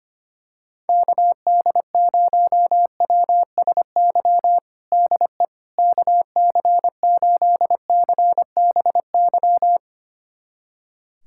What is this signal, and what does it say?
Morse code, heard as a keyed tone, about 25 wpm, 700 Hz. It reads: KD0WSY DE KC8CBY
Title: USA Callsign DE Callsign 25wpm